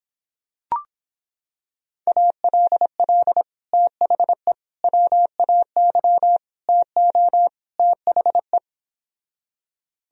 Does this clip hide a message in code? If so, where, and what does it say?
Yes: Morse code from 2.1 to 8.6 s, heard as a keyed tone, about 26 wpm, 700 Hz. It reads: ALL THE WAY TO THE